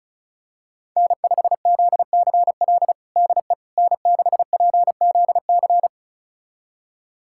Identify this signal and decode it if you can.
Morse code, heard as a keyed tone, about 36 wpm, 700 Hz. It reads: N5ZCL DE N6PZC
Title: USA Callsign DE Callsign 35wpm